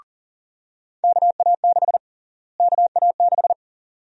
Morse code, heard as a keyed tone, about 40 wpm, 700 Hz. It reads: KA6 KA6
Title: USA Call Sign Prefixes - Repeated 2x - 40wpm